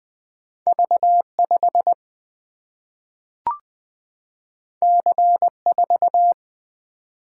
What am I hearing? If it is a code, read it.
Morse code, heard as a keyed tone, about 20 wpm, 700 Hz. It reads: V5 C4
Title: Single-Letter Single-Number 20wpm